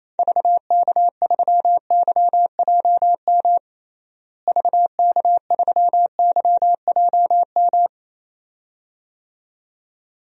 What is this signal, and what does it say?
Morse code, heard as a keyed tone, about 27 wpm, 700 Hz. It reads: VK3YJM VK3YJM
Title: DXCC-100 Call Signs - 6 Characters - Repeated 2x - 28wpm